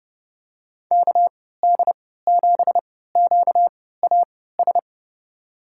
Morse code, heard as a keyed tone, about 30 wpm, spaced as for 17 wpm, 700 Hz. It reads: KD7QAS
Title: USA Call Signs - 6 Characters - 3x Character Spacing - 30wpm